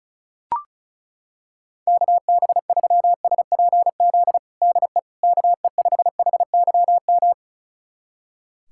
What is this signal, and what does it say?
Morse code, heard as a keyed tone, about 36 wpm, 700 Hz. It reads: KB3SPZ DE KE5HYM
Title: USA Callsign DE Callsign - 6 Characters - 35wpm